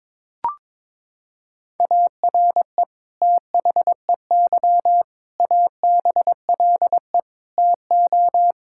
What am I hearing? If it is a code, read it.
Morse code, heard as a keyed tone, about 22 wpm, 700 Hz. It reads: ARE THEY ABLE TO
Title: Sentences from Top 1000 Words - Review Part 5 - 22wpm